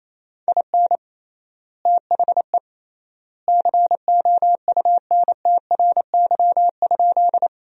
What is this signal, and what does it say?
Morse code, heard as a keyed tone, about 27 wpm, 700 Hz. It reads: IN THE COUNTRY?
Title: 3x Word-Spacing - Sentences from Top 300 Words - 28wpm